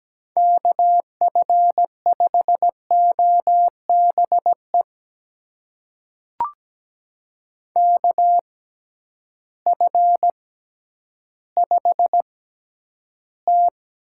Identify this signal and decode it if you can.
Morse code, heard as a keyed tone, about 17 wpm, spaced as for 10 wpm, 700 Hz. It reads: KF5OBE KF5T
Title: USA Call Signs - 6 Characters - 6x Character Spacing - 17wpm